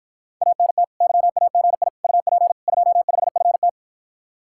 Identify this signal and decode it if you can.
Morse code, heard as a keyed tone, about 54 wpm, 700 Hz. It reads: ANT YAGI UP 25FT